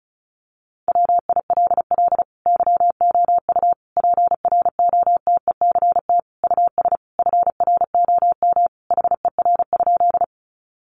Morse code, heard as a keyed tone, about 34 wpm, 700 Hz. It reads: WILL YOU PROTECT US FROM HER?
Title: Sentences from Top 900 Words 35wpm